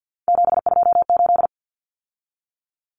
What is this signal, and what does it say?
Morse code, heard as a keyed tone, about 50 wpm, 700 Hz. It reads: 728
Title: Common Cut Numbers - 3-Digit 50wpm